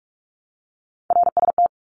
Morse code, heard as a keyed tone, about 44 wpm, 700 Hz. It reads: RST
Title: Abbreviations Repeated 3x - 45wpm